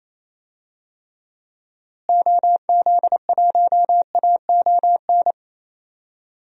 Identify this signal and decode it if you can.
Morse code, heard as a keyed tone, about 28 wpm, 700 Hz. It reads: OZ1AON
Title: DXCC-100 Call Signs - 6 Characters - Repeated 3x - 28wpm